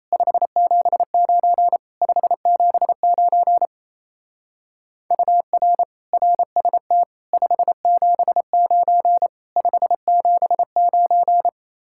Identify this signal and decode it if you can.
Morse code, heard as a keyed tone, about 34 wpm, 700 Hz. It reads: 579 579 UR RST 579 579
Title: Speed-Racing Common QSO Phrases - 28wpm